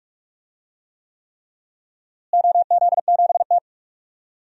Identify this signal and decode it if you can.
Morse code, heard as a keyed tone, about 46 wpm, 700 Hz. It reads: OZ7T
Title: DXCC-100 Call Signs - 4 Characters - Repeated 3x - 45wpm